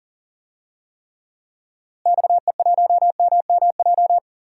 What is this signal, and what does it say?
Morse code, heard as a keyed tone, about 40 wpm, 700 Hz. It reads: XE1MMJ
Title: DXCC-100 Call Signs - 6 Characters - Repeated 2x - 40wpm